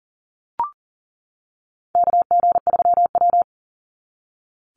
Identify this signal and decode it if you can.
Morse code, heard as a keyed tone, about 40 wpm, 700 Hz. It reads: KG3W
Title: NA SOTA Chaser Callsigns 40wpm